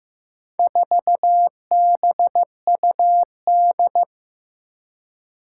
Morse code, heard as a keyed tone, about 15 wpm, 700 Hz. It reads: 4BUD
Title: Four Challenging Letter-Number Combinations 15wpm